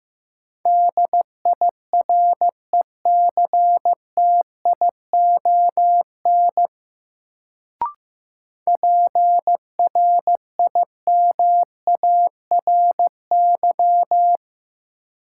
Morse code, heard as a keyed tone, about 15 wpm, 700 Hz. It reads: DIRECTION PRIMARY
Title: Sentences from Top 1000 Words - Review Part 5 - New Word Drill - 15wpm